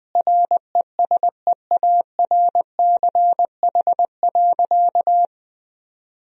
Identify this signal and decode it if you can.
Morse code, heard as a keyed tone, about 20 wpm, 700 Hz. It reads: RESEARCH.
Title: Sentences from Top 1000 Words - Review Part 4 - 20wpm